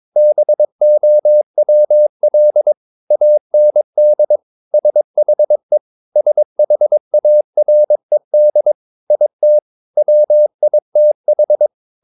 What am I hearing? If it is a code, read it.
Morse code, heard as a keyed tone, about 22 wpm, 600 Hz. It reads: BOWL AND SHE SHARED IT WITH
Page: Romanen Never af Ken Follet på engelsk.